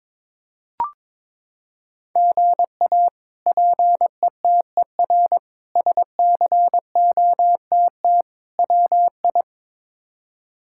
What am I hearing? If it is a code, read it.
Morse code, heard as a keyed tone, about 22 wpm, 700 Hz. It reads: GA PETER SCOTT WI
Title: K1USN SST QSO Practice - 22wpm